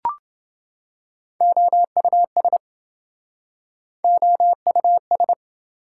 Morse code, heard as a keyed tone, about 27 wpm, 700 Hz. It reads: OUS OUS